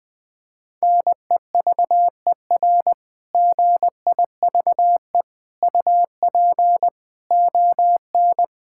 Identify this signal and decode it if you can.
Morse code, heard as a keyed tone, about 20 wpm, 700 Hz. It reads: NEVER GIVE UP ON
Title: Difficult Sentences with Hints - 20wpm